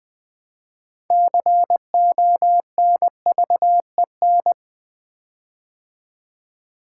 Morse code, heard as a keyed tone, about 20 wpm, 700 Hz. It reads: CONVEN
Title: Build A Word - 6 Letters At A Time - 12-Letter Words - 20wpm